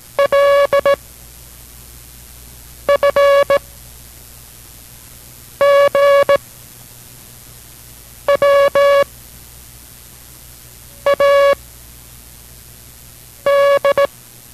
Morse code, heard as a keyed tone, about 17 wpm, 560 Hz. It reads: L F G W A D